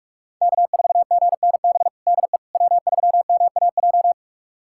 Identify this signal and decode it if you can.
Morse code, heard as a keyed tone, about 42 wpm, 700 Hz. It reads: K4GNB DE W3MAJ
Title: USA Callsign DE Callsign - 5 Characters - 45wpm